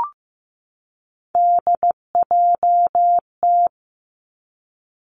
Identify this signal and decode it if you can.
Morse code, heard as a keyed tone, about 15 wpm, 700 Hz. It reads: DJT
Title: Three Challenging Letter-Number Combinations 15wpm